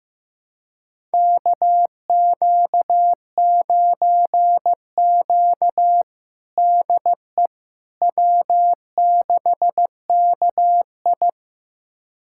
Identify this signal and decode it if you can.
Morse code, heard as a keyed tone, about 16 wpm, 700 Hz. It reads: KQ9Q DE W6KI
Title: USA Callsign DE Callsign - 4 Characters - 15wpm